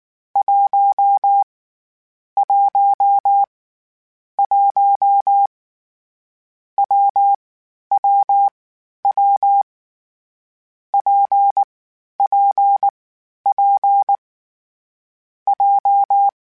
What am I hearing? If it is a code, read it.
Morse code, heard as a keyed tone, about 19 wpm, 800 Hz. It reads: 1 1 1 W W W P P P J
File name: Difficult Combos / Difficult Combos-mp3s / WPJ WPJ1 @18